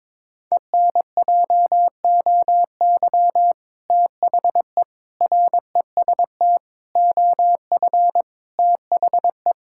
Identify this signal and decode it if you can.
Morse code, heard as a keyed tone, about 22 wpm, 700 Hz. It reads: ENJOY THE REST OF THE
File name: practice / 22wpm / Sentences with 2-Word Phrases - 22wpm